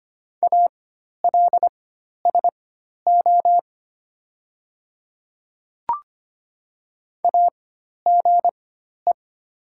Morse code, heard as a keyed tone, about 25 wpm, spaced as for 12 wpm, 700 Hz. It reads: ALSO AGE
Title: 4x Character Spacing - Top 500 Words - 25wpm